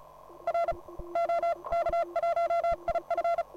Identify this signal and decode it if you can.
Morse code, heard as a keyed tone, about 33 wpm, 680 Hz. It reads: R OK1IF